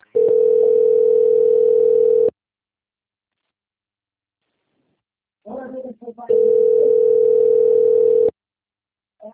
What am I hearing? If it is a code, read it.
Ringback tone (North America), 440 + 480 Hz, with speech.
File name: out- - - - - .40981